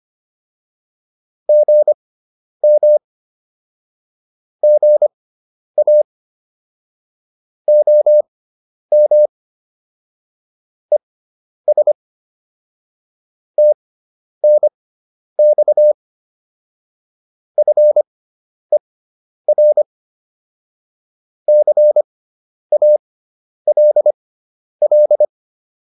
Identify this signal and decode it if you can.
Morse code, heard as a keyed tone, about 25 wpm, spaced as for 10 wpm, 600 Hz. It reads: GM GA OM ES TNX FER CALL
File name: QSO407 10